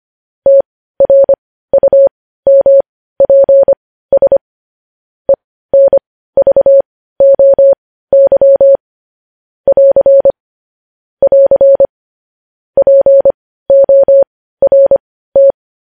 Morse code, heard as a keyed tone, about 25 wpm, spaced as for 15 wpm, 550 Hz. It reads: TRUMPS ENVOY <AR> <AR> PORT
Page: news-headlines-in-morse-code-at-15-wpm